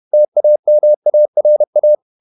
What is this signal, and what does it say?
Morse code, heard as a keyed tone, about 31 wpm, 600 Hz. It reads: TAMARA
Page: Never 31wpm | CW med Gnister